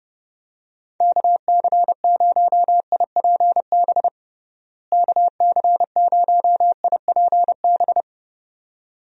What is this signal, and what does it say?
Morse code, heard as a keyed tone, about 30 wpm, 700 Hz. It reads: KC0IPB KC0IPB
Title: USA Call Signs - 6 Characters - Repeated 2x - 30wpm